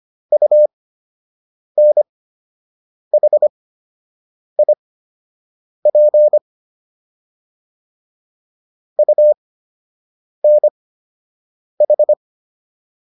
Morse code, heard as a keyed tone, about 25 wpm, spaced as for 7 wpm, 600 Hz. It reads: UNHIP UNH